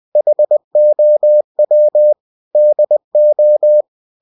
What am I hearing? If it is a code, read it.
Morse code, heard as a keyed tone, about 20 wpm, 600 Hz. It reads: HOW DO
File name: cw-books / winnie-the-pooh / winnie-20wpm-page-0064